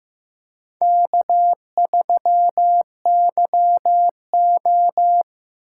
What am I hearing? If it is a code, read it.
Morse code, heard as a keyed tone, about 17 wpm, 700 Hz. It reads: K3YO
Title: Speed Racing - USA 1x2 Call Signs 15wpm